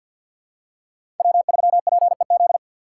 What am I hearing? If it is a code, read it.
Morse code, heard as a keyed tone, about 53 wpm, 700 Hz. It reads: W3PEZ
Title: USA Call Signs - 5 Characters - 6x Character Spacing - 50wpm